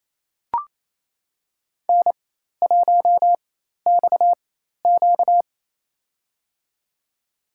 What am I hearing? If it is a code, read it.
Morse code, heard as a keyed tone, about 27 wpm, spaced as for 13 wpm, 700 Hz. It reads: N1XQ
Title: USA Call Signs - 4 Characters - 4x Character Spacing - 28wpm